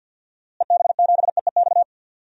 Morse code, heard as a keyed tone, about 52 wpm, 700 Hz. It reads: EB7EEX
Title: DXCC-100 Call Signs - 6 Characters - Repeated 3x - 50wpm